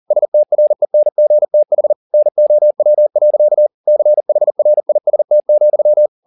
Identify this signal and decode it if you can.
Morse code, heard as a keyed tone, about 40 wpm, 600 Hz. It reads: STRENGTH NOW. CHRIST,